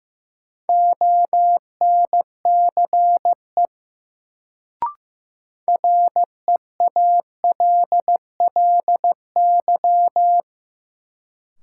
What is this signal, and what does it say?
Morse code, heard as a keyed tone, about 16 wpm, 700 Hz. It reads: ONCE REALLY
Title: Sets of 4 Words - New Word Drill - 15wpm